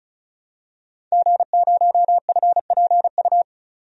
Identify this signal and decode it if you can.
Morse code, heard as a keyed tone, about 34 wpm, 700 Hz. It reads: G0FPU
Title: DXCC-100 Call Signs - 5 Characters - 35wpm